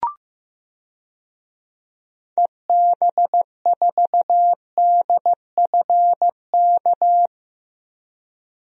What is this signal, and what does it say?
Morse code, heard as a keyed tone, about 15 wpm, 700 Hz. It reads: EB4DFK
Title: DXCC-100 Call Signs - 6 Characters - 15wpm